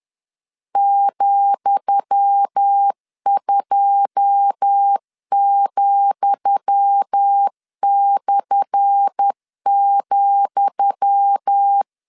Morse code, heard as a keyed tone, about 10 wpm, 800 Hz. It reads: ,2,/,
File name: LEZIONI / 10 SIGNS / 10D COMA 10wpm 3013